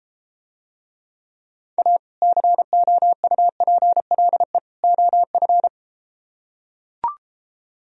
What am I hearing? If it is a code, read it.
Morse code, heard as a keyed tone, about 33 wpm, 700 Hz. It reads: A COUPLE OF